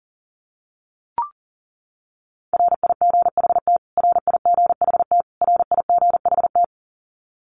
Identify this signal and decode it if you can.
Morse code, heard as a keyed tone, about 40 wpm, 700 Hz. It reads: RIGHT RIGHT RIGHT
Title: Top 500 Words Repeated 40wpm